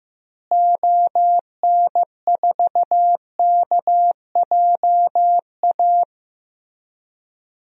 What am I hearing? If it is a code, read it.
Morse code, heard as a keyed tone, about 15 wpm, 700 Hz. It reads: ON4KJA
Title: DXCC-100 Call Signs - 6 Characters - Repeated 2x - 15wpm